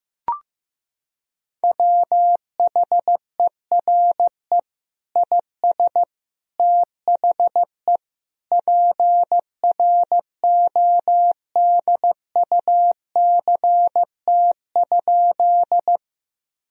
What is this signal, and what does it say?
Morse code, heard as a keyed tone, about 15 wpm, 700 Hz. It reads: WHERE IS THE PRODUCT?
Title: Sentences from Top 1000 Words - Review Part 2 - 15wpm